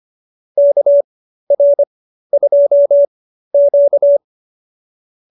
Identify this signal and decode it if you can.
Morse code, heard as a keyed tone, about 25 wpm, spaced as for 13 wpm, 590 Hz. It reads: KR2Q